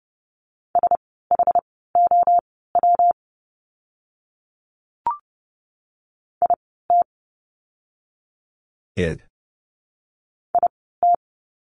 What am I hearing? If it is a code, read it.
Morse code, heard as a keyed tone, about 30 wpm, spaced as for 17 wpm, 700 Hz. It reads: SHOW IT IT
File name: course / 30:10wpm / 230 Lesson 37f - Letter x - Words 30wpm